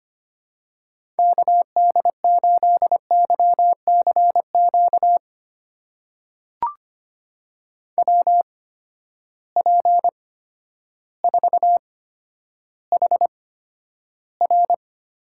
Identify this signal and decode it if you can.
Morse code, heard as a keyed tone, about 25 wpm, spaced as for 11 wpm, 700 Hz. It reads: KD8YCQ WP4HR
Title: USA Call Signs - 6 Characters - 8x Character Spacing - 25wpm